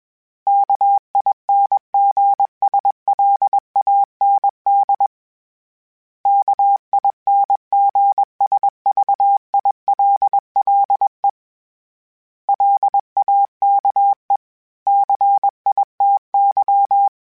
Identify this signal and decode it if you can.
Morse code, heard as a keyed tone, about 21 wpm, 800 Hz. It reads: KINGSLAND KINGSVILLE LAKE CITY
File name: fromSMs / FromSMs mp3s / Cities@20